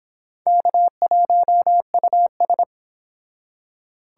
Morse code, heard as a keyed tone, about 29 wpm, 700 Hz. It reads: K1US